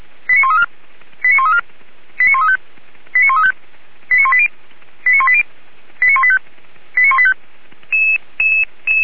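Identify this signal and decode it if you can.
Five-tone selective call (ZVEI), heard as quick runs of five tones, tones of about 70 ms. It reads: 89134 89134 89135 89135 89180 89180 89185 89185
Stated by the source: ZVEI five-tone selective call 89132 received by SDR, recorded on the feuerwehr-orlamuende fire brigade server.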